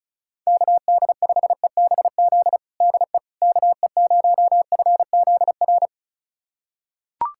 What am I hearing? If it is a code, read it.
Morse code, heard as a keyed tone, about 34 wpm, 690 Hz. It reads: KD5EBZ DE KE0FZR
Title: USA Callsign DE Callsign - 6 Characters - 35wpm